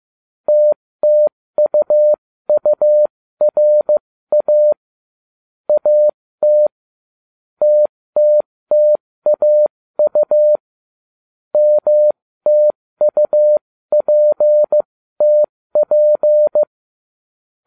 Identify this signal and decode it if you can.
Morse code, heard as a keyed tone, about 15 wpm, spaced as for 13 wpm, 600 Hz. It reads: TTUURA AT TTTAU MTUPTP